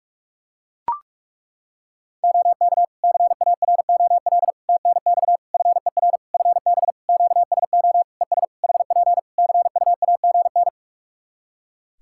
Morse code, heard as a keyed tone, about 41 wpm, 700 Hz. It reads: OK CAROL TNX FER FB QSO ES HP CUAGN
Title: Practice QSO - Exchange 4 - Ending - 45wpm